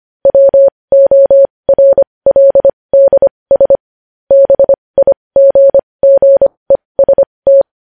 Morse code, heard as a keyed tone, about 25 wpm, spaced as for 20 wpm, 550 Hz. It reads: WORLDS BIGGEST
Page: news-headlines-in-morse-code-at-20-wpm